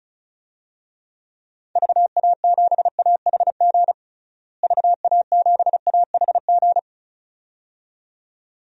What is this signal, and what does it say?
Morse code, heard as a keyed tone, about 34 wpm, 700 Hz. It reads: VA7AHG VA7AHG
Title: DXCC-100 Call Signs - 6 Characters - Repeated 2x - 35wpm